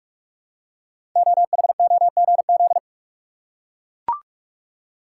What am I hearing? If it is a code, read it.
Morse code, heard as a keyed tone, about 44 wpm, 700 Hz. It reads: OHOGZ
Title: Code Groups - 5 Letters - 45wpm